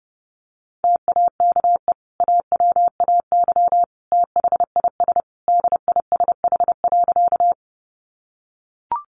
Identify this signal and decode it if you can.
Morse code, heard as a keyed tone, about 30 wpm, 700 Hz. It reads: TAKE AWAY THIS DISH.
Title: Sentences from Top 1300 Words 30wpm